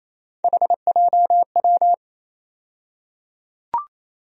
Morse code, heard as a keyed tone, about 27 wpm, 700 Hz. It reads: HJW
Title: USA Call Sign Postfixes - 3 Letters 28wpm